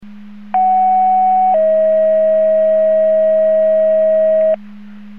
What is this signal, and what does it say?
Two-tone page, heard as a one-second tone, then a three-second tone, about 750 then 640 Hz. It is heard from 0.5 to 4.6 s.
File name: Portage 154.385 Paging